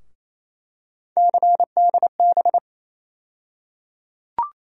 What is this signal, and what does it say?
Morse code, heard as a keyed tone, about 28 wpm, 700 Hz. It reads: CDB
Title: USA Call Sign Postfixes - 3 Letters 28wpm